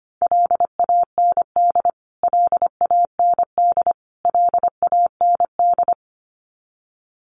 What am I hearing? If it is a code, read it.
Morse code, heard as a keyed tone, about 25 wpm, 700 Hz. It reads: LAND LAND LAND
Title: Top 300 Words Repeated 25wpm